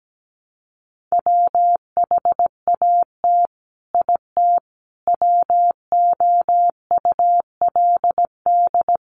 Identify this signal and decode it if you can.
Morse code, heard as a keyed tone, about 17 wpm, 700 Hz. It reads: WHAT IT WOULD
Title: Sets of 5 Words Encore 17wpm